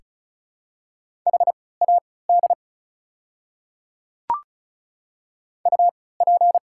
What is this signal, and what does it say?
Morse code, heard as a keyed tone, about 34 wpm, spaced as for 20 wpm, 700 Hz. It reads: HAD UP
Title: 3x Character Spacing - Top 300 Words - 35wpm